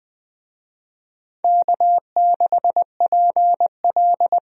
Morse code, heard as a keyed tone, about 21 wpm, 700 Hz. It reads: K6PL
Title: USA Call Signs - 4 Characters - Repeated 2x - 20wpm